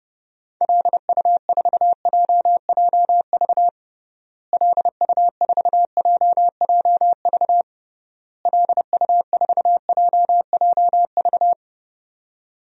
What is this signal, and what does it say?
Morse code, heard as a keyed tone, about 30 wpm, 700 Hz. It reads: LU4JJV LU4JJV LU4JJV
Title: DXCC-100 Call Signs - 6 Characters - Repeated 3x - 30wpm